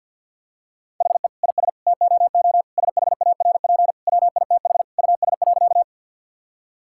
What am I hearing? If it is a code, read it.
Morse code, heard as a keyed tone, about 50 wpm, 700 Hz. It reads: HE IS TOO SHARP WITH US.
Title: Sentences from Top 1600 Words 50wpm